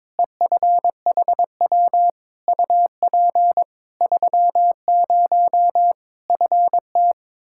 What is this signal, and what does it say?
Morse code, heard as a keyed tone, about 22 wpm, 700 Hz. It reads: EFHW UP 30 FT
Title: Practice QSO - Exchange 2 - Info - 22wpm